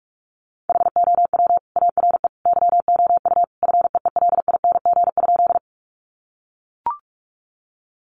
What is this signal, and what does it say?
Morse code, heard as a keyed tone, about 44 wpm, 700 Hz. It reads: HOW ARE YOU FEELING?
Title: Sentences from Top 1200 Words 45wpm